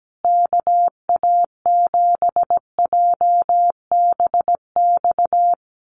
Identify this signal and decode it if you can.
Morse code, heard as a keyed tone, about 17 wpm, 700 Hz. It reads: KA7JBX
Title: USA Call Signs Repeated 3x 17wpm